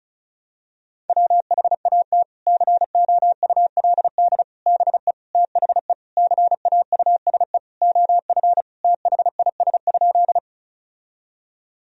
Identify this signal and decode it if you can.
Morse code, heard as a keyed tone, about 36 wpm, 700 Hz. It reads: WHAT COULD BE THE CAUSE OF THIS?
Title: Sentences from Top 1000 Words - Review Part 1 - 35wpm